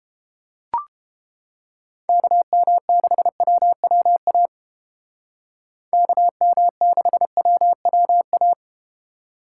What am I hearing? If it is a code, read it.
Morse code, heard as a keyed tone, about 31 wpm, 700 Hz. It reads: KM6WWA KM6WWA